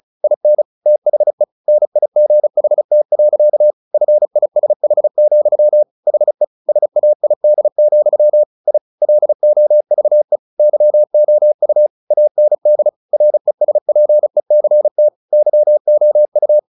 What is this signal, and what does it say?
Morse code, heard as a keyed tone, about 34 wpm, 600 Hz. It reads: IN THE NIGHT. FISH, HE SAID, I LOVE YOU AND RESPECT YOU
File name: the old man and the sea / the old man and the sea 1 35wpm